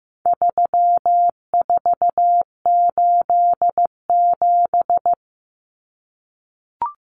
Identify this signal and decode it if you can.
Morse code, heard as a keyed tone, about 15 wpm, 700 Hz. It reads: 3487
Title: Numbers - 4-Digit 15wpm